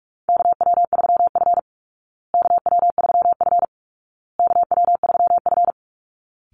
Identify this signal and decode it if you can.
Morse code, heard as a keyed tone, about 44 wpm, 700 Hz. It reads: KW3F KW3F KW3F
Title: Call Signs Repeated 2x - 45wpm